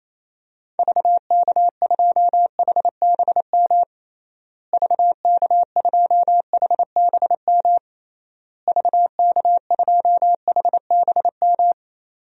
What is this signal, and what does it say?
Morse code, heard as a keyed tone, about 27 wpm, 700 Hz. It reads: VK2HBM VK2HBM VK2HBM
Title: DXCC-100 Call Signs - 6 Characters - Repeated 3x - 28wpm